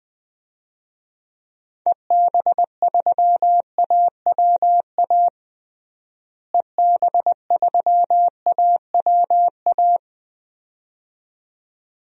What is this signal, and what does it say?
Morse code, heard as a keyed tone, about 20 wpm, 700 Hz. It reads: EB3AWA EB3AWA
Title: DXCC-100 Call Signs - 6 Characters - Repeated 2x - 20wpm